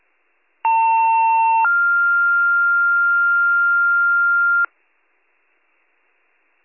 Two-tone page, heard as a one-second tone, then a three-second tone, about 900 then 1400 Hz. It is heard from 0.6 to 4.7 s.